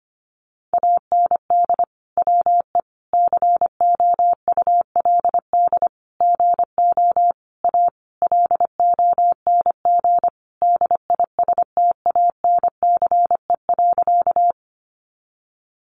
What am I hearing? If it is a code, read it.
Morse code, heard as a keyed tone, about 25 wpm, 700 Hz. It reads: AND WE COULD GO A LONG DISTANCE.
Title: Sentences from Top 1200 Words 25wpm